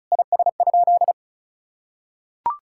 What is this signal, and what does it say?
Morse code, heard as a keyed tone, about 36 wpm, 700 Hz. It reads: IS?
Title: Sentences from Top 1000 Words - Review Part 5 - 35wpm